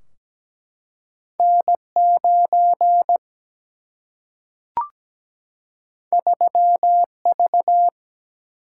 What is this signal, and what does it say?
Morse code, heard as a keyed tone, about 17 wpm, 700 Hz. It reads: N 9 3 V
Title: Single-Letter Single-Number 17wpm